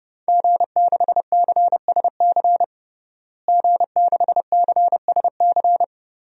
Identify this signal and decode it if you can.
Morse code, heard as a keyed tone, about 30 wpm, 700 Hz. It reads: G6CSC G6CSC
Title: DXCC-100 Call Signs - 5 Characters - Repeated 3x - 30wpm